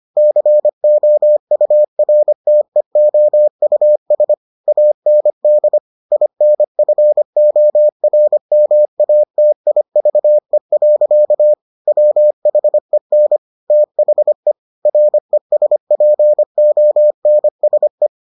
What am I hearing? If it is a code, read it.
Morse code, heard as a keyed tone, about 25 wpm, 600 Hz. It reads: COURTEOUS AND INFORMATIVE. WHEN THE RESPONSE